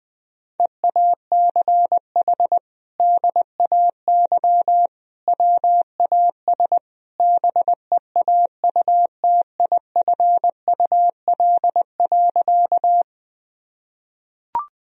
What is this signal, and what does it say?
Morse code, heard as a keyed tone, about 19 wpm, 700 Hz. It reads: EACH DAY WAS BEAUTIFUL.
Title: Sentences from Top 1000 Words - Review Part 1 - 20wpm